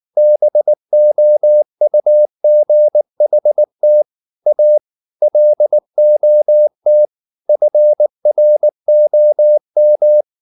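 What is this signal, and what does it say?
Morse code, heard as a keyed tone, about 19 wpm, 600 Hz. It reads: BOUGHT A LOT FROM
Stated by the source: Morse code at 19 wpm from CW med Gnister, reading Ken Follett's novel Never in English.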